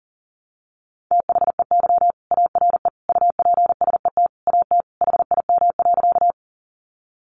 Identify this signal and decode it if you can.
Morse code, heard as a keyed tone, about 38 wpm, 700 Hz. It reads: THEY ARE UPSET AT HIM.
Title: Sentences from Top 800 Words 40wpm